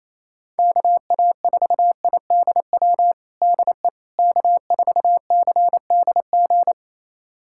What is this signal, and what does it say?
Morse code, heard as a keyed tone, about 28 wpm, 700 Hz. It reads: KA4IDW DE K4CDG
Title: USA Callsign DE Callsign 28wpm